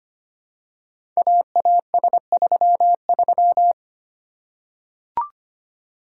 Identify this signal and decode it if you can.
Morse code, heard as a keyed tone, about 25 wpm, 700 Hz. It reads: AAS33
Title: Code Groups - 5 Letter:Numbers - 25wpm